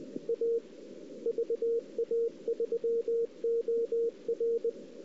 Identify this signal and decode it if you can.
Morse code, heard as a keyed tone, about 20 wpm, 460 Hz. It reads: A VA3OR
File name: CW / 2008-ss-cw-k5tr-op-n5rz -03:00